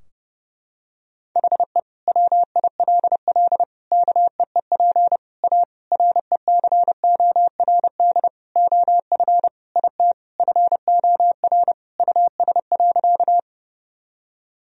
Morse code, heard as a keyed tone, about 30 wpm, 700 Hz. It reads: HE WILL KEEP A RECORD OF IT FOR US.
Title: Sentences from Top 1000 Words - Review Part 3 - 30wpm